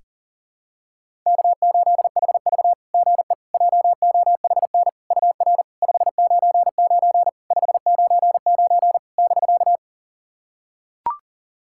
Morse code, heard as a keyed tone, about 38 wpm, 690 Hz. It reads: K8HV GE JOHN UR 599 599 <BK>
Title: SOTA QSO Practice 40wpm